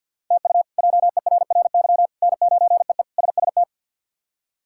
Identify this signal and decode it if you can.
Morse code, heard as a keyed tone, about 49 wpm, 700 Hz. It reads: TU JERRY N9EE SST
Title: K1USN SST QSO Practice - 50wpm